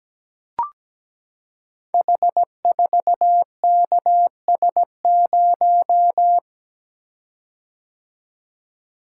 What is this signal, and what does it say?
Morse code, heard as a keyed tone, about 17 wpm, 700 Hz. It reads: H4KS0
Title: Code Groups - 5 Letter:Numbers - 17wpm